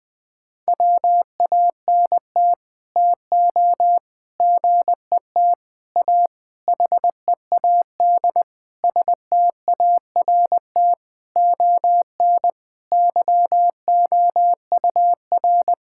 Morse code, heard as a keyed tone, about 20 wpm, 700 Hz. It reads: WANT TO GET A HEAD START ON YOUR
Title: Sets of 9 Words - 20wpm